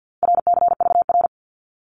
Morse code, heard as a keyed tone, about 50 wpm, 700 Hz. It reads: RCVR
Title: Abbreviations 50wpm